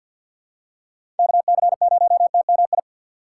Morse code, heard as a keyed tone, about 52 wpm, 700 Hz. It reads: KC0TMI
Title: USA Call Signs - 6 Characters - 4x Character Spacing - 50wpm